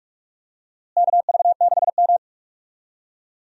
Morse code, heard as a keyed tone, about 42 wpm, 700 Hz. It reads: KV6M
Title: USA Call Signs - 4 Characters - Repeated 2x - 45wpm